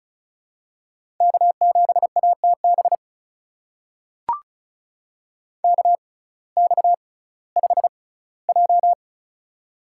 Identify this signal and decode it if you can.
Morse code, heard as a keyed tone, about 36 wpm, spaced as for 19 wpm, 700 Hz. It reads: K7ATB KX5J
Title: USA Call Signs - 6x Character Spacing - 35wpm